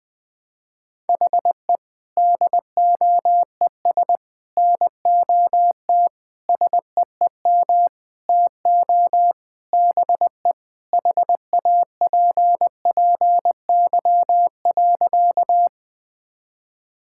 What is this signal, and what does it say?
Morse code, heard as a keyed tone, about 21 wpm, 700 Hz. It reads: HE DOES NOT SEEM TO BE HAPPY.
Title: Sentences with 5-Word Phrases - 20wpm